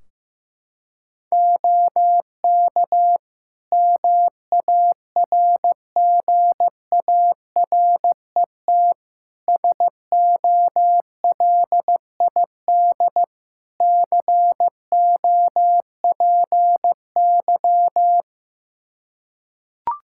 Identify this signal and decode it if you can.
Morse code, heard as a keyed tone, about 15 wpm, 700 Hz. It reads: OK MARGARET SOLID COPY
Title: Practice QSO - Exchange 3 - More Info - 15wpm